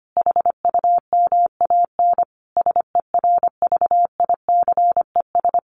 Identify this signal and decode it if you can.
Morse code, heard as a keyed tone, about 25 wpm, 700 Hz. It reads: HUMAN SERVICES
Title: Sets of 5 Words 25wpm